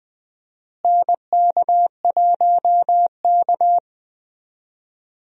Morse code, heard as a keyed tone, about 20 wpm, 700 Hz. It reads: NK1K
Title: USA Call Signs - 4 Characters - Repeated 2x - 20wpm